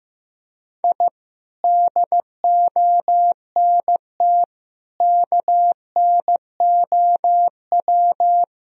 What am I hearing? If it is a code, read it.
Morse code, heard as a keyed tone, about 15 wpm, 700 Hz. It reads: I DONT KNOW
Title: Sentences from Top 1000 Words - Review Part 3 - 15wpm